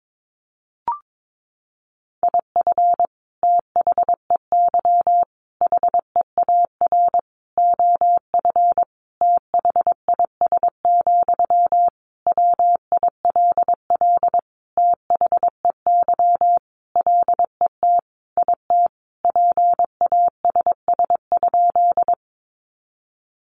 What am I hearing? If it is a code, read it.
Morse code, heard as a keyed tone, about 22 wpm, 700 Hz. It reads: IF THEY HEAR OF THIS, WILL THEY LET IT PASS?
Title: Sentences from Top 400 Words 22wpm